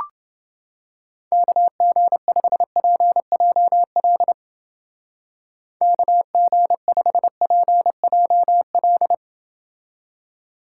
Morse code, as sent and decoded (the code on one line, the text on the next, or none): -.- --. ..... .--. .--- .-.. / -.- --. ..... .--. .--- .-..
KG5PJL KG5PJL